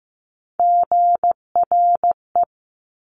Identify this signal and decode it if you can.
Morse code, heard as a keyed tone, about 15 wpm, 700 Hz. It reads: GRE
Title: Top Three Letter Combinations in Words 15wpm